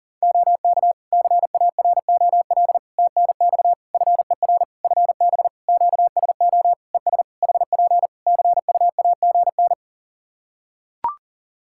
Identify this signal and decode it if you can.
Morse code, heard as a keyed tone, about 41 wpm, 700 Hz. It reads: OK CAROL TNX FER FB QSO ES HP CUAGN
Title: Practice QSO - Exchange 4 - Ending - 40wpm